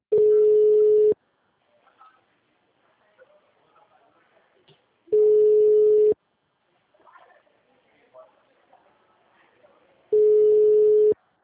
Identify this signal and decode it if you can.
Ringback tone (Europe, 425 Hz), with speech.
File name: out- -2001- - - .42702